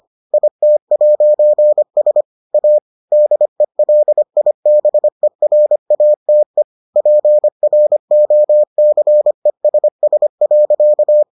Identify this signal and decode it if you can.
Morse code, heard as a keyed tone, about 25 wpm, 600 Hz. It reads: IT'S A DELIBERATE PROCESS.